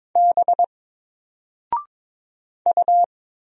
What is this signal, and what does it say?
Morse code, heard as a keyed tone, about 22 wpm, 700 Hz. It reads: B U
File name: Single-Letter-Number-22wpm